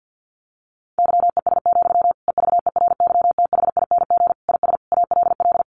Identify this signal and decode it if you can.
Morse code, heard as a keyed tone, about 50 wpm, 700 Hz. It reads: YES, EVERYTHING IS ALL
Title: Sentences from Top 1100 Words 50wpm